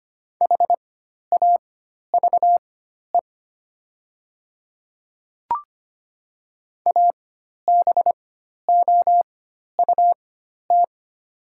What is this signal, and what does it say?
Morse code, heard as a keyed tone, about 25 wpm, spaced as for 12 wpm, 700 Hz. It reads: HAVE ABOUT
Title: 4x Character Spacing - Top 100 Words - 25wpm